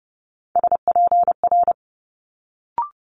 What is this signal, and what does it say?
Morse code, heard as a keyed tone, about 30 wpm, 700 Hz. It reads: SPR
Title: Multi-letter Phonemes Quick Fire 30wpm